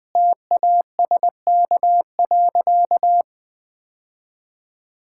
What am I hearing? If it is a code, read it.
Morse code, heard as a keyed tone, about 20 wpm, 700 Hz. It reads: TASK.
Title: Difficult Sentences with Hints - 20wpm